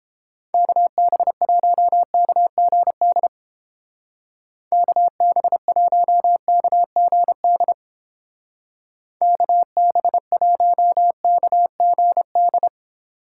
Morse code, heard as a keyed tone, about 30 wpm, 700 Hz. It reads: KB1KGD KB1KGD KB1KGD